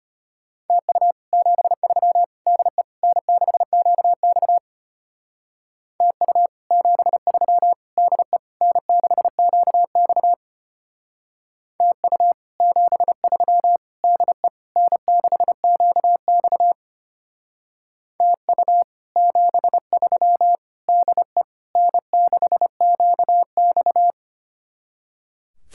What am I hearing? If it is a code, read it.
Morse code, heard as a keyed tone, about 31 wpm, 700 Hz. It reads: TU 73 DE N6QX TU 73 DE N6QX TU 73 DE N6QX TU 73 DE N6QX
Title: Speed-Racing Common QSO Phrases - 25wpm